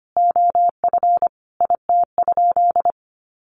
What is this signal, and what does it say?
Morse code, heard as a keyed tone, about 25 wpm, 700 Hz. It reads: OF IT?
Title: Sentences from Top 1400 Words 25wpm